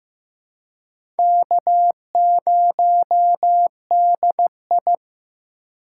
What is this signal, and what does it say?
Morse code, heard as a keyed tone, about 15 wpm, 700 Hz. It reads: K0DI
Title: USA Call Signs - 4 Characters - 4x Character Spacing - 15wpm